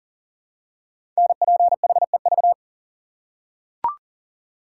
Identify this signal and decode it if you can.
Morse code, heard as a keyed tone, about 42 wpm, 700 Hz. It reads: NPHEV
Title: Code Groups - 5 Letters - 40wpm